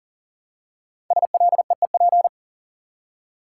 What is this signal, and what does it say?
Morse code, heard as a keyed tone, about 40 wpm, 700 Hz. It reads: SLEEP
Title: Five Letter Words - 40wpm